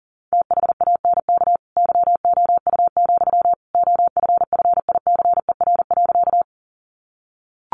Morse code, heard as a keyed tone, about 40 wpm, 700 Hz. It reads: THANK YOU, OFFICER.
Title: Sentences from Top 1200 Words 40wpm